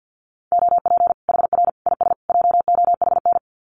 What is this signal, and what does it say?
Morse code, heard as a keyed tone, about 48 wpm, 700 Hz. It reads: OP HR IS JOHN
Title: Common QSO Phrases 50wpm